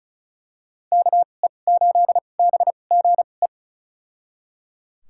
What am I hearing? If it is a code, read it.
Morse code, heard as a keyed tone, about 34 wpm, spaced as for 25 wpm, 700 Hz. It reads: KE8BGE
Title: USA Call Signs - 6 Characters - 2x Character Spacing - 35wpm